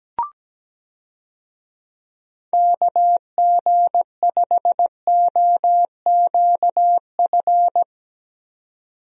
Morse code, heard as a keyed tone, about 17 wpm, 700 Hz. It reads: KG5OQF
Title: USA Call Signs - 6 Characters - 17wpm